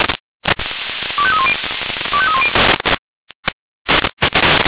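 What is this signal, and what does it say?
Five-tone selective call (ZVEI), heard as quick runs of five tones, tones of about 70 ms. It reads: 25310 25310
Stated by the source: ZVEI five-tone call received by SDR.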